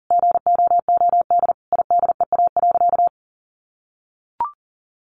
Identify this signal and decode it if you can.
Morse code, heard as a keyed tone, about 40 wpm, 700 Hz. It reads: GOOD IDEA.
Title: Sentences from Top 600 Words 40wpm